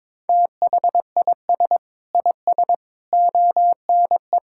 Morse code, heard as a keyed tone, about 22 wpm, 700 Hz. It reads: THIS IS ONE
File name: Sentences with 7-Word Phrases - 22wpm